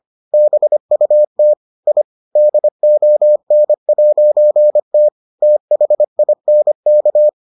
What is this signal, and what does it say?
Morse code, heard as a keyed tone, about 25 wpm, 600 Hz. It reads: BUT I DON'T THINK